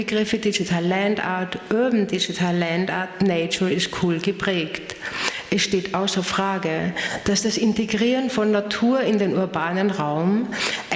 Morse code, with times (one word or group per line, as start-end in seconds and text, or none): none